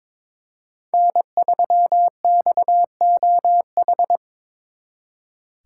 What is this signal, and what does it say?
Morse code, heard as a keyed tone, about 26 wpm, 700 Hz. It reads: N3XOH
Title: Speed Racing - USA 1x3 Call Signs 22wpm